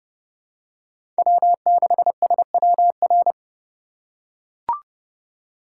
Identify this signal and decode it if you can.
Morse code, heard as a keyed tone, about 30 wpm, 700 Hz. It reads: W6SWR
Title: USA Call Signs - 4x Character Spacing - 30wpm